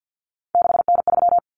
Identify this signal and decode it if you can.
Morse code, heard as a keyed tone, about 50 wpm, 700 Hz. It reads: 6N3
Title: Common Cut Numbers - 3-Digit 50wpm